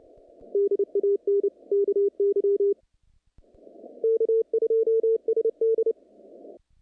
Morse code, heard as a keyed tone, about 26 wpm, 440 Hz. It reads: DAN KY K2SD
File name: contests / naqp-cw-k5tr-op-k5pi- -2200